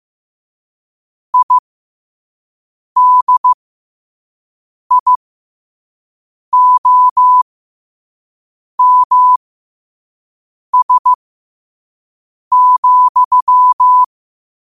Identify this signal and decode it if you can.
Morse code, heard as a keyed tone, about 15 wpm, spaced as for 5 wpm, 1000 Hz. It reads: IDIOMS,
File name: qotdfiles / 5wpm